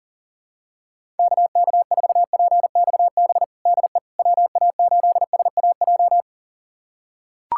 Morse code, heard as a keyed tone, about 40 wpm, 700 Hz. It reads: KK4PXB DE WA8SAJ
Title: USA Callsign DE Callsign - 6 Characters - 40wpm